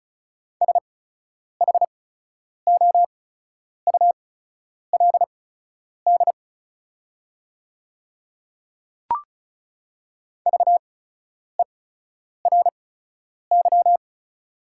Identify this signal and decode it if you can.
Morse code, heard as a keyed tone, about 34 wpm, spaced as for 10 wpm, 700 Hz. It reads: SHOULD VERY
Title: 8x Character Spacing - Top 500 Words - 35wpm